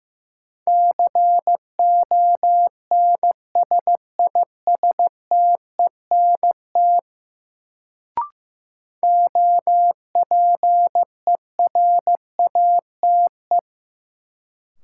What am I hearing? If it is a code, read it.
Morse code, heard as a keyed tone, about 15 wpm, 700 Hz. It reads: CONSISTENT OPERATE
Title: Sentences from Top 1500 Words - New Word Drill Alt - 15wpm